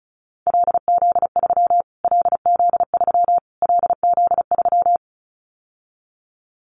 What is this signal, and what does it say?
Morse code, heard as a keyed tone, about 34 wpm, 700 Hz. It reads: LZ3 LZ3 LZ3
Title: Callsign Prefixes for Top 25 DXCC - Repeated 3x - 35wpm